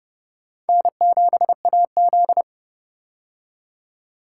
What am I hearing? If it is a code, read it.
Morse code, heard as a keyed tone, about 30 wpm, 700 Hz. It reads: N7AZ
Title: K1USN SST QSO Practice - 30wpm